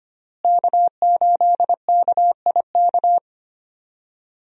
Morse code, heard as a keyed tone, about 25 wpm, 700 Hz. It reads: K8KIK
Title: USA Call Signs - 5 Characters - Repeated 2x - 25wpm